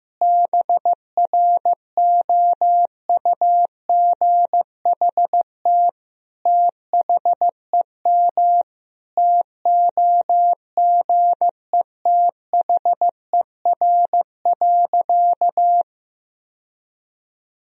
Morse code, heard as a keyed tone, about 15 wpm, 700 Hz. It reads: BROUGHT THEM TOGETHER.
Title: Sentences from Top 1000 Words - Review Part 3 - 15wpm